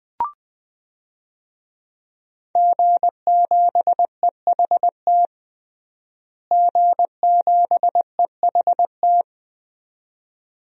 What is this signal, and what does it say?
Morse code, heard as a keyed tone, about 20 wpm, 700 Hz. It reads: G7EHT G7EHT
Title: DXCC-100 Call Signs - 5 Characters - Repeated 2x - 20wpm